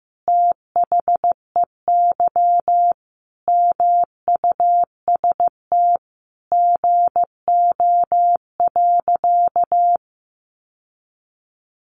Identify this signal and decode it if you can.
Morse code, heard as a keyed tone, about 15 wpm, 700 Hz. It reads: THEY MUST GO.
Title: Sentences from Top 1200 Words 15wpm